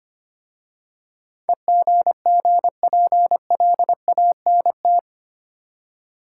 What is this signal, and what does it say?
Morse code, heard as a keyed tone, about 25 wpm, 700 Hz. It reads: EGGPLANT
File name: practice / 25wpm / Easy Words with Hints - 25wpm